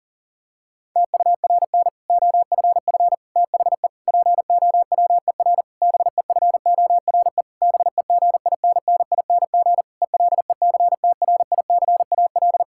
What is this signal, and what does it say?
Morse code, heard as a keyed tone, about 40 wpm, 700 Hz. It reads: TURN OFF THE POWER BEFORE BEGINNING ELECTRICAL
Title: Sentences from Top 1000 Words - Review Part 6 - 40wpm